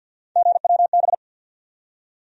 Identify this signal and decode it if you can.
Morse code, heard as a keyed tone, about 52 wpm, 700 Hz. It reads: GWB
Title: USA Call Sign Postfixes - 3 Letters 50wpm